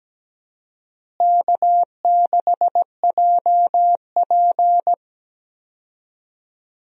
Morse code, heard as a keyed tone, about 17 wpm, 700 Hz. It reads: K6JP
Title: USA Call Signs - 4 Characters - Repeated 3x - 17wpm